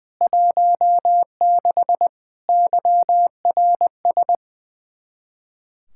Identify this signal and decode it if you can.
Morse code, heard as a keyed tone, about 20 wpm, 700 Hz. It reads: 16 YRS
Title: Practice QSO - Exchange 3 - More Info - 20wpm